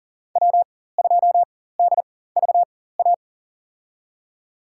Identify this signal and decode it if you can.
Morse code, heard as a keyed tone, about 40 wpm, spaced as for 19 wpm, 700 Hz. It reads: W2DVA
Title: USA Call Signs - 5 Characters - 4x Character Spacing - 40wpm